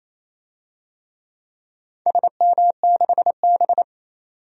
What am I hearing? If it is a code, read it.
Morse code, heard as a keyed tone, about 28 wpm, 700 Hz. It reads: SM6B
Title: DXCC-100 Call Signs - 4 Characters - 28wpm